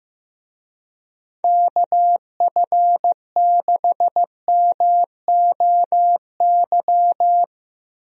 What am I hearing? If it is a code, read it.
Morse code, heard as a keyed tone, about 15 wpm, 700 Hz. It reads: KF6MOY
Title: USA Call Signs - 6 Characters - 15wpm